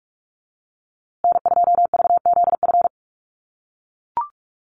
Morse code, heard as a keyed tone, about 44 wpm, 700 Hz. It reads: N2VZF
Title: USA Call Signs Repeated 3x 45wpm